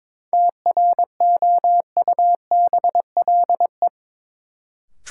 Morse code, heard as a keyed tone, about 21 wpm, 700 Hz. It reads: TROUBLE
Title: Sentences from Top 1200 Words - New Word Drill Alt - 22wpm